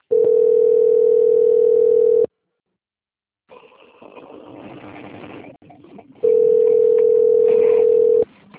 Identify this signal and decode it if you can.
Ringback tone (North America), 440 + 480 Hz.
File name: out- - - - - .83600